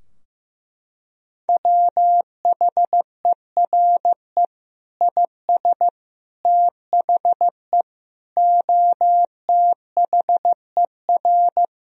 Morse code, heard as a keyed tone, about 15 wpm, 700 Hz. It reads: WHERE IS THE OTHER
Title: Sentences from Top 1000 Words - Review Part 2 - 15wpm